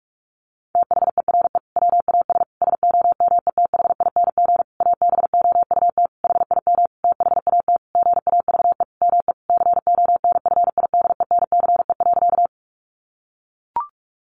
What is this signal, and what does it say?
Morse code, heard as a keyed tone, about 44 wpm, 700 Hz. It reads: THERE WAS SOMETHING ABOUT HIM THAT GAVE ME CONFIDENCE.
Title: Sentences from Top 1500 Words 45wpm